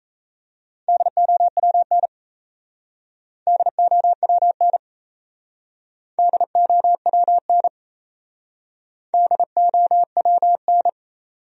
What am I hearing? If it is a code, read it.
Morse code, heard as a keyed tone, about 36 wpm, 700 Hz. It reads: DOWN DOWN DOWN DOWN